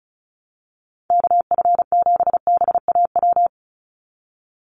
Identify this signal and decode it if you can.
Morse code, heard as a keyed tone, about 34 wpm, 700 Hz. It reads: KF7BAW
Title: USA Call Signs 35wpm